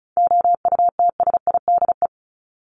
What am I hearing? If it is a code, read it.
Morse code, heard as a keyed tone, about 34 wpm, 700 Hz. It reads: OUTSIDE
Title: Top 751-1000 Words 35wpm